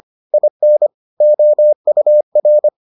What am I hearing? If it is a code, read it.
Morse code, heard as a keyed tone, about 25 wpm, 600 Hz. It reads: IN OUR